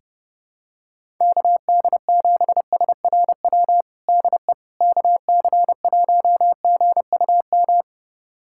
Morse code, heard as a keyed tone, about 28 wpm, 700 Hz. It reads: KD7SRW DE KC1GUM
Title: USA Callsign DE Callsign - 6 Characters - 30wpm